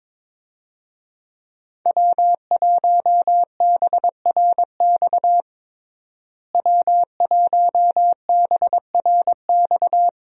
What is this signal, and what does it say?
Morse code, heard as a keyed tone, about 22 wpm, 690 Hz. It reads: W1BRX W1BRX
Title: USA Call Signs - 5 Characters - Repeated 3x - 22wpm